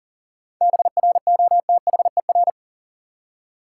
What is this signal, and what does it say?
Morse code, heard as a keyed tone, about 40 wpm, 700 Hz. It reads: BROTHER
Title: Seven Letter Words - 40wpm